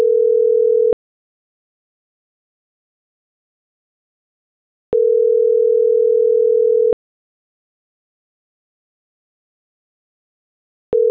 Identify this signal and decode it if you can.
Ringback tone (North America), 440 + 480 Hz.
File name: CENTRAL NMP / out melanie.diaznmp